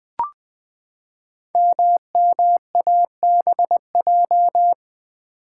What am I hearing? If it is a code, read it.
Morse code, heard as a keyed tone, about 20 wpm, 700 Hz. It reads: MMABJ
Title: Code Groups - 5 Letter:Numbers - 20wpm